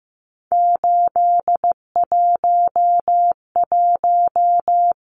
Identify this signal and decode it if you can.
Morse code, heard as a keyed tone, about 15 wpm, 700 Hz. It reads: 811
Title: Numbers - 3-Digit 15wpm